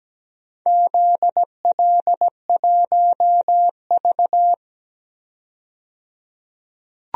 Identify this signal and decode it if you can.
Morse code, heard as a keyed tone, about 17 wpm, 700 Hz. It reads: ZL1V
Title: DXCC-100 Call Signs - 4 Characters - 17wpm